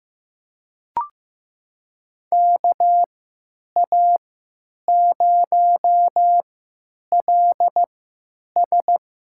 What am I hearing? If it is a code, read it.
Morse code, heard as a keyed tone, about 15 wpm, spaced as for 9 wpm, 700 Hz. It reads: KA0LS
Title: USA Call Signs - 6 Characters - 3x Character Spacing - 15wpm